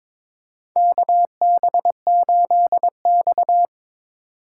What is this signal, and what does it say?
Morse code, heard as a keyed tone, about 22 wpm, 700 Hz. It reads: KB8X
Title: USA Call Signs - 4 Characters - 8x Character Spacing - 22wpm